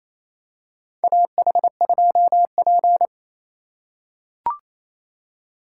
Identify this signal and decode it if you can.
Morse code, heard as a keyed tone, about 27 wpm, 700 Hz. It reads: AH2P
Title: USA Call Signs - 4 Characters - 2x Character Spacing - 28wpm